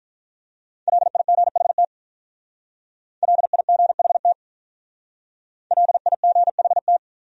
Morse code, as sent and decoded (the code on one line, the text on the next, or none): .-.. .. --. .... - / .-.. .. --. .... - / .-.. .. --. .... -
LIGHT LIGHT LIGHT